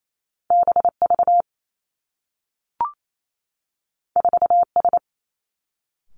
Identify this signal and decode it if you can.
Morse code, heard as a keyed tone, about 28 wpm, 700 Hz. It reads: B V 4 S
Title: Two Challenging Letter-Number Combinations 28wpm